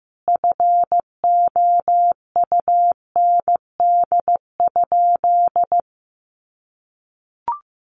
Morse code, heard as a keyed tone, about 15 wpm, 700 Hz. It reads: FOUND?
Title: Sentences from Top 300 Words 15wpm